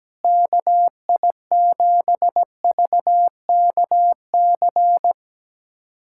Morse code, heard as a keyed tone, about 17 wpm, 700 Hz. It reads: KI7VKC
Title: USA Callsign DE Callsign - 6 Characters - 17wpm